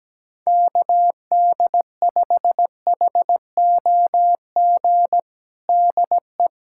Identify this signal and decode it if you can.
Morse code, heard as a keyed tone, about 17 wpm, 700 Hz. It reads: KD5HOG DE
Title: USA Callsign DE Callsign - 6 Characters - 17wpm